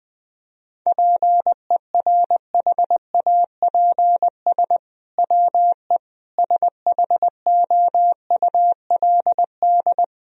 Morse code, heard as a keyed tone, about 21 wpm, 700 Hz. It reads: PERHAPS WE SHOULD
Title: Sentences from Top 1000 Words - Review Part 4 - 20wpm